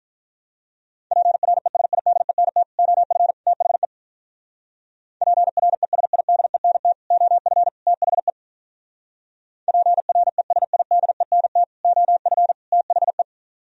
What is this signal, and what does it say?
Morse code, heard as a keyed tone, about 43 wpm, 700 Hz. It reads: PRESIDENT OF THE PRESIDENT OF THE PRESIDENT OF THE
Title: Speed-Racing - Sets of 3 Words 35wpm